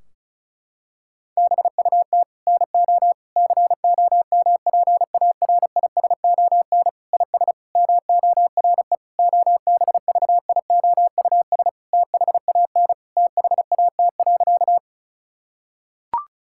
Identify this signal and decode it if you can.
Morse code, heard as a keyed tone, about 34 wpm, 700 Hz. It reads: BUT NO COMPARISON IS MORE OBVIOUS THAN THAT.
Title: Sentences from Top 1600 Words 35wpm